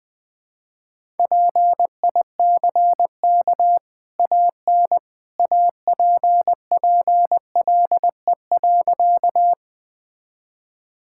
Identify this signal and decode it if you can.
Morse code, heard as a keyed tone, about 20 wpm, 700 Hz. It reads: PICK AN APPLE.
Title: Easy Sentences with Hints - 20wpm